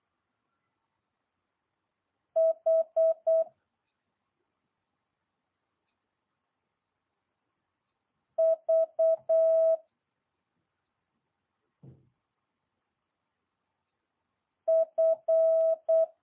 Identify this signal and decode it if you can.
Morse code, heard as a keyed tone, about 8 wpm, 650 Hz. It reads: H V F
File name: jh8cbh